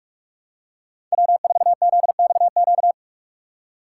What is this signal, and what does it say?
Morse code, heard as a keyed tone, about 42 wpm, 700 Hz. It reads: W4ZXQ
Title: USA Call Signs - 5 Characters - 3x Character Spacing - 45wpm